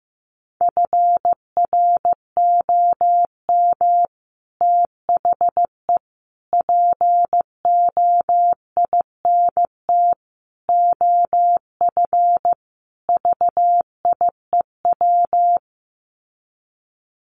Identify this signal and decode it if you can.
Morse code, heard as a keyed tone, about 14 wpm, 700 Hz. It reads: FROM THE POINT OF VIEW
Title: Sets of 5 Words 15wpm